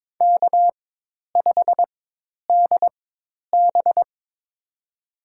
Morse code, heard as a keyed tone, about 21 wpm, spaced as for 10 wpm, 700 Hz. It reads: K5DB
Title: USA Call Signs - 4 Characters - 4x Character Spacing - 22wpm